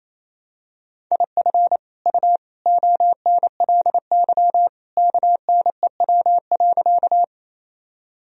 Morse code, heard as a keyed tone, about 27 wpm, 700 Hz. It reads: IF U ONLY KNEW.
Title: Sentences from Top 400 Words and Abbreviations 28wpm